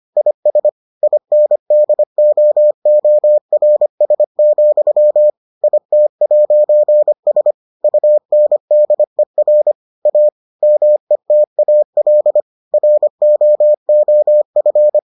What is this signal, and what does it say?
Morse code, heard as a keyed tone, about 25 wpm, 600 Hz. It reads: IS INDOORS, IT'S UNDER A METAL ROOF